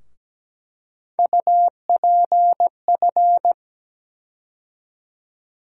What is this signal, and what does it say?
Morse code, heard as a keyed tone, about 17 wpm, 700 Hz. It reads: UPF
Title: USA Call Sign Postfixes - 3 Letters 17wpm